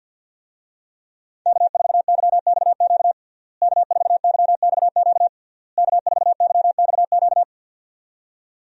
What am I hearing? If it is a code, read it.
Morse code, heard as a keyed tone, about 48 wpm, 700 Hz. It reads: K4YXQ K4YXQ K4YXQ
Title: USA Call Signs - 5 Characters - Repeated 3x - 50wpm